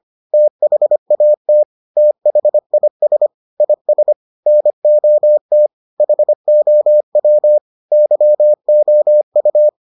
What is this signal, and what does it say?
Morse code, heard as a keyed tone, about 25 wpm, 600 Hz. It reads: THAT THIS IS NOT HOW YOU